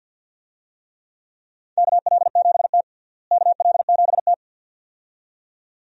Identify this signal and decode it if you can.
Morse code, heard as a keyed tone, about 52 wpm, 700 Hz. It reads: KL7T KL7T
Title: DXCC-100 Call Signs - 4 Characters - Repeated 2x - 50wpm